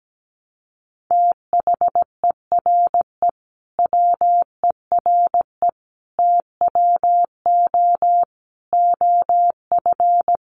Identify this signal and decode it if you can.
Morse code, heard as a keyed tone, about 17 wpm, 700 Hz. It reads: THERE WERE TWO OF
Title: Sentences from Top 100 Words 17wpm